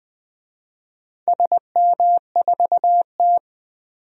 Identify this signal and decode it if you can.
Morse code, heard as a keyed tone, about 20 wpm, 700 Hz. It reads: SM4T
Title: DXCC-100 Call Signs - 4 Characters - 20wpm